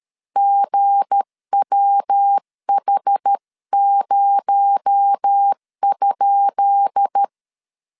Morse code, heard as a keyed tone, about 13 wpm, 800 Hz. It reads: GWH0?
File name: LEZIONI / 10 SIGNS / 10B INT 12wpm 2414